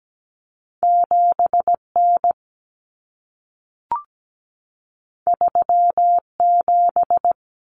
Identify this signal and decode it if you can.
Morse code, heard as a keyed tone, about 17 wpm, 700 Hz. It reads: 7 N 3 7
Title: Common Cut Numbers - 2-Digit 17wpm